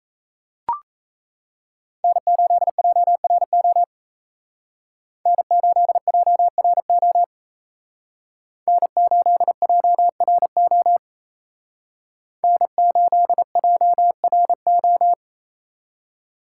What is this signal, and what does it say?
Morse code, heard as a keyed tone, about 33 wpm, 700 Hz. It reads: N8JRO N8JRO N8JRO N8JRO
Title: Speed Racing - USA 1x3 Call Signs 28wpm